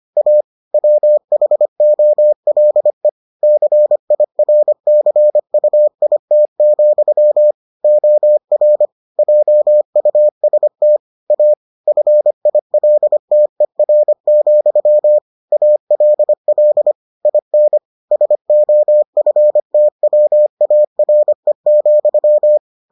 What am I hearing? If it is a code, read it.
Morse code, heard as a keyed tone, about 25 wpm, 600 Hz. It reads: A WHOLE CIRCUIT, OR JUST A FILTER, ALL IN SOFTWARE,